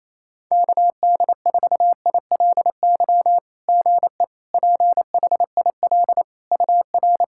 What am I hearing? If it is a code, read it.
Morse code, heard as a keyed tone, about 27 wpm, 690 Hz. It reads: KD4ILY GE PHIL UR
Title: SOTA QSO Practice 28wpm